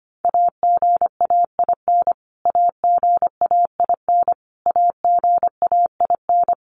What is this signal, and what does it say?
Morse code, heard as a keyed tone, about 25 wpm, 700 Hz. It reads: AGAIN AGAIN AGAIN
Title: Top 300 Words Repeated 25wpm